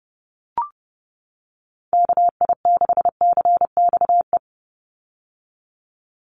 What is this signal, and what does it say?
Morse code, heard as a keyed tone, about 30 wpm, 700 Hz. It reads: KI6CXE
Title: USA Call Signs 30wpm